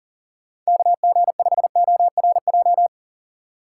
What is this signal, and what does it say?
Morse code, heard as a keyed tone, about 40 wpm, 700 Hz. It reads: KG5ORJ
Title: USA Call Signs - 6 Characters - 2x Character Spacing - 40wpm